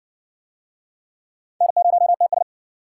Morse code, heard as a keyed tone, about 55 wpm, 700 Hz. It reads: N0TS